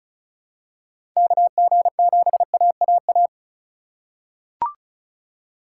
Morse code, heard as a keyed tone, about 34 wpm, 700 Hz. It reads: KG7AAA
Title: USA Call Signs - 2x Character Spacing - 35wpm